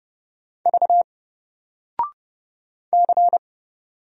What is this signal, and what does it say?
Morse code, heard as a keyed tone, about 30 wpm, 700 Hz. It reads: V C
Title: Single Letter-Number 30wpm